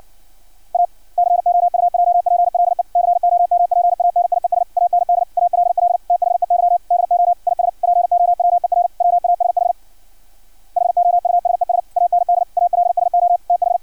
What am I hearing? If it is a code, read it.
Morse code, heard as a keyed tone, about 56 wpm, 700 Hz. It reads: A COUJPLE COMPANIES AND ALL THEY DO IS POWER WASH HOUSES AND ALSO TH
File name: archived / CRSnet